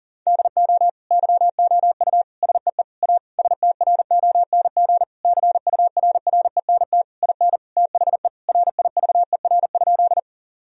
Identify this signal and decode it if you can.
Morse code, heard as a keyed tone, about 40 wpm, 700 Hz. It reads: DO YOU SEE A STRONG CURRENT IN THE RIVER?
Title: Sentences from Top 1000 Words - Review Part 4 - 40wpm